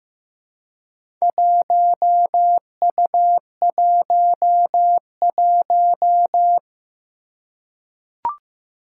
Morse code, heard as a keyed tone, about 15 wpm, 700 Hz. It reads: 1U11
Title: Four Challenging Letter-Number Combinations 15wpm